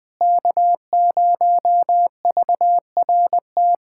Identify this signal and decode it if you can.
Morse code, heard as a keyed tone, about 21 wpm, 700 Hz. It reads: K0VRT
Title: USA Callsign DE Callsign - 5 Characters - 20wpm